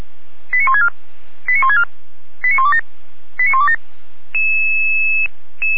Five-tone selective call (ZVEI), heard as quick runs of five tones, tones of about 70 ms. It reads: 89164 89164 89127 89127